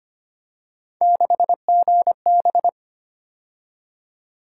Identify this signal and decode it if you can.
Morse code, heard as a keyed tone, about 29 wpm, 700 Hz. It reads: 6GB
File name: Speed-Racing - Three Letter-Number Combinations - 25wpm